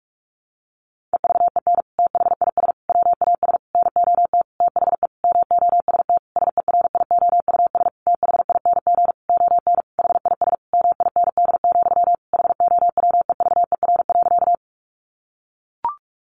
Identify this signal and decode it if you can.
Morse code, heard as a keyed tone, about 44 wpm, 700 Hz. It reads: EVEN THIS WAS NOT THE MOST SERIOUS THING ON HIS MIND, HOWEVER.
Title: Sentences from Top 1100 Words 45wpm